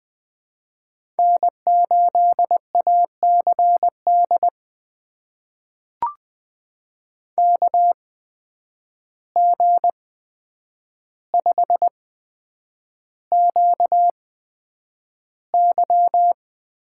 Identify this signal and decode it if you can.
Morse code, heard as a keyed tone, about 20 wpm, spaced as for 9 wpm, 700 Hz. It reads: N8ACD KG5QY
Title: USA Call Signs - 5 Characters - 8x Character Spacing - 20wpm